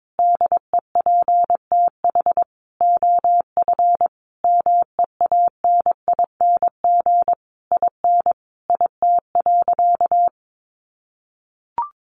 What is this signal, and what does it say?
Morse code, heard as a keyed tone, about 22 wpm, 700 Hz. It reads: DEPTH OF MEANING IN IT.
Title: Sentences from Top 1200 Words 22wpm